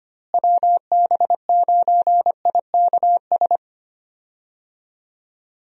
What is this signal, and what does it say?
Morse code, heard as a keyed tone, about 25 wpm, 700 Hz. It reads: WB9IKS
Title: USA Call Signs - 6 Characters - Repeated 2x - 25wpm